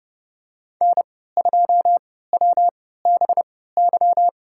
Morse code, heard as a keyed tone, about 30 wpm, spaced as for 17 wpm, 700 Hz. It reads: N2WBY
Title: USA Call Signs - 5 Characters - 3x Character Spacing - 30wpm